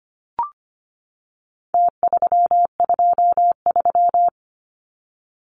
Morse code, heard as a keyed tone, about 25 wpm, 700 Hz. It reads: T323
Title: Numbers - Contest Style - 4-Digit 25wpm